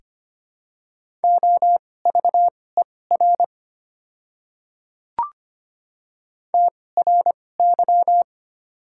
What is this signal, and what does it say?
Morse code, heard as a keyed tone, about 25 wpm, spaced as for 18 wpm, 700 Hz. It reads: OVER TRY
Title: 2x Character Spacing - Top 200 Words - 25wpm